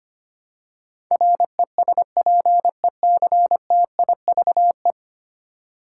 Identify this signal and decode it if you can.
Morse code, heard as a keyed tone, about 25 wpm, 700 Hz. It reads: RESPECTIVE
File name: 10 Letter Words - First 3 Letters Given - 25wpm